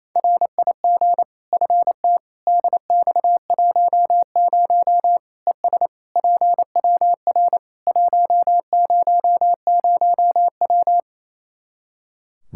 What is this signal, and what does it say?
Morse code, heard as a keyed tone, about 28 wpm, 700 Hz. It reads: RIG FT DX10 ES PWR 100W
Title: Practice QSO - Exchange 2 - Info - 28wpm